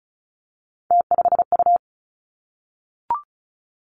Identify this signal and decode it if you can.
Morse code, heard as a keyed tone, about 36 wpm, 700 Hz. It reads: T5U
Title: Three Challenging Letter-Number Combinations 35wpm